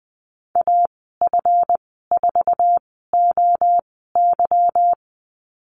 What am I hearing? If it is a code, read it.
Morse code, heard as a keyed tone, about 20 wpm, spaced as for 15 wpm, 700 Hz. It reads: AF4OY
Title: 186 Lesson 30g - Letter b - Call Signs 20wpm